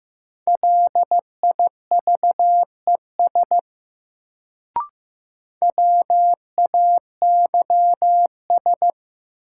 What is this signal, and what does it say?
Morse code, heard as a keyed tone, about 15 wpm, 700 Hz. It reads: LIVES WAYS
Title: Sets of 5 Words Encore - New Word Drill - 15wpm